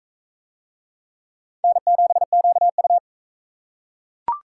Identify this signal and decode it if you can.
Morse code, heard as a keyed tone, about 41 wpm, 700 Hz. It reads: N7QU